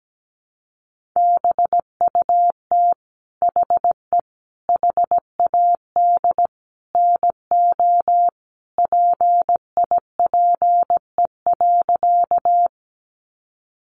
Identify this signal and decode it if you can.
Morse code, heard as a keyed tone, about 17 wpm, 700 Hz. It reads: BUT HE HAD NO PIPE.
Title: Sentences from Top 1500 Words 17wpm